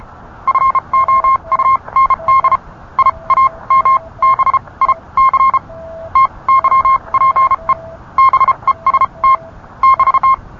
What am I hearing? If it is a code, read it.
Morse code, heard as a keyed tone, about 32 wpm, spaced as for 25 wpm, 1060 Hz. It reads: FOUND IAMBIC TYPE BEST <BT>